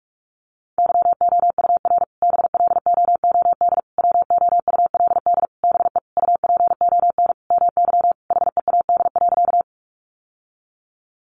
Morse code, heard as a keyed tone, about 44 wpm, 700 Hz. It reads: YOUR BLOOD WOULD BE UPON MY HEAD.
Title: Sentences from Top 1100 Words 45wpm